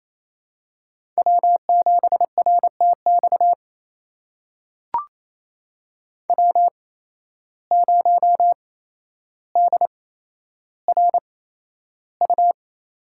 Morse code, heard as a keyed tone, about 28 wpm, spaced as for 12 wpm, 700 Hz. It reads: W7RTX W0DRU
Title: USA Call Signs - 5 Characters - 8x Character Spacing - 28wpm